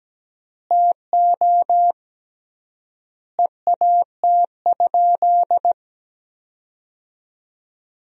Morse code, heard as a keyed tone, about 17 wpm, 700 Hz. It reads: TO EAT?
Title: 3x Word-Spacing - Sentences from Top 300 Words - 17wpm